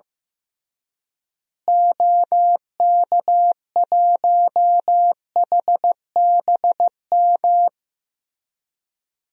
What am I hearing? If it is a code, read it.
Morse code, heard as a keyed tone, about 15 wpm, 700 Hz. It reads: OK1HBM
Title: DXCC-100 Call Signs - 6 Characters - Repeated 2x - 15wpm